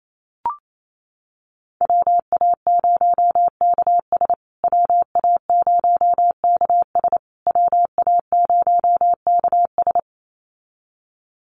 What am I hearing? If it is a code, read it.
Morse code, heard as a keyed tone, about 28 wpm, 700 Hz. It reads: WA0KS WA0KS WA0KS
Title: USA Call Signs Repeated 3x 28wpm